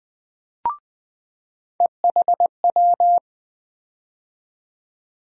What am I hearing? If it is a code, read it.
Morse code, heard as a keyed tone, about 20 wpm, 700 Hz. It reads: EHW
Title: USA Call Sign Postfixes - 3 Letters 20wpm